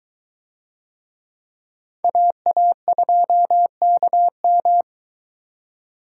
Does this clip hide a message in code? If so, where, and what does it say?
Yes: Morse code from 2.0 to 4.8 s, heard as a keyed tone, about 22 wpm, 700 Hz. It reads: AA2KM